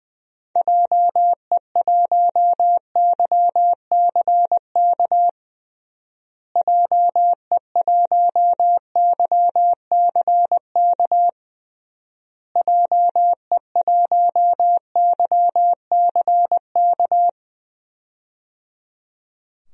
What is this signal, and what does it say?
Morse code, heard as a keyed tone, about 20 wpm, 700 Hz. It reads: JE1YCK JE1YCK JE1YCK
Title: DXCC-100 Call Signs - 6 Characters - Repeated 3x - 20wpm